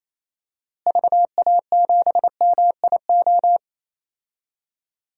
Morse code, heard as a keyed tone, about 27 wpm, 700 Hz. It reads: VA7MIO
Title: DXCC-100 Call Signs - 6 Characters - Repeated 3x - 28wpm